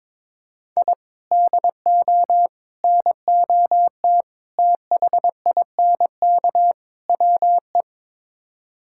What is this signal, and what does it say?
Morse code, heard as a keyed tone, about 22 wpm, 700 Hz. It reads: I DO NOT THINK WE
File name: practice / 22wpm / Build A Phrase - 5-Word Phrases - 22wpm